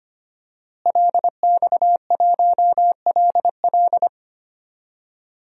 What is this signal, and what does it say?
Morse code, heard as a keyed tone, about 25 wpm, 700 Hz. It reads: LX1LL
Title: DXCC-100 Call Signs - 5 Characters - Repeated 2x - 25wpm